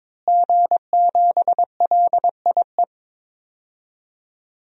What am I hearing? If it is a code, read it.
Morse code, heard as a keyed tone, about 22 wpm, 700 Hz. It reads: G7LIE
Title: DXCC-100 Call Signs - 5 Characters - Repeated 2x - 22wpm